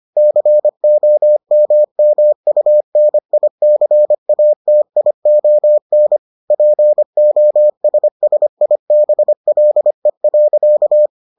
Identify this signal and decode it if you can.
Morse code, heard as a keyed tone, about 25 wpm, 600 Hz. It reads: COMMUNICATION POSSIBLE.